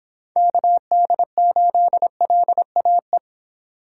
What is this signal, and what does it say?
Morse code, heard as a keyed tone, about 25 wpm, 700 Hz. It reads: KD8LAE